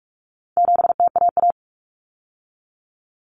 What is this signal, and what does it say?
Morse code, heard as a keyed tone, about 44 wpm, 700 Hz. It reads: 7TAA
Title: Common Cut Numbers - 4-Digit 45wpm